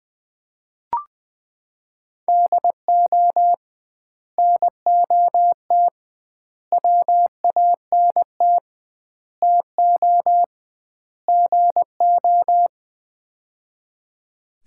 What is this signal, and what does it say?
Morse code, heard as a keyed tone, about 20 wpm, 700 Hz. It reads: DO NOT WANT TO GO
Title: 2x Word-Spacing - Sets of 5 Words - 20wpm